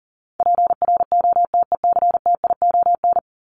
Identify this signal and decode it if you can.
Morse code, heard as a keyed tone, about 40 wpm, 700 Hz. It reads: PROTECTION
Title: Top 1501-1750 Words 40wpm